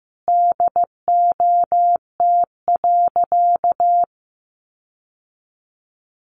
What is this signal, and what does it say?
Morse code, heard as a keyed tone, about 15 wpm, 700 Hz. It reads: DOT.
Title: Sentences from Top 1500 Words 15wpm